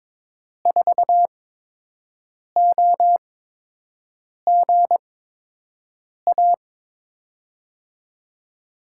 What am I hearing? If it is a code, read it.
Morse code, heard as a keyed tone, about 22 wpm, spaced as for 6 wpm, 700 Hz. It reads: 4OGA
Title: USA Call Signs - 5 Characters - 8x Character Spacing - 22wpm